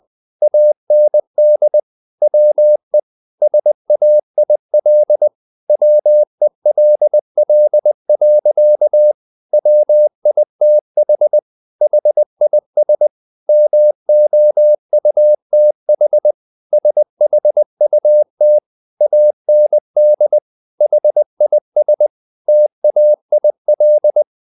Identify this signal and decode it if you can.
Morse code, heard as a keyed tone, about 20 wpm, 600 Hz. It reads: AND WE SAIL WELL. WITH HIS MOUTH SHUT AND HIS TAIL
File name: the old man and the sea part 2 20wpm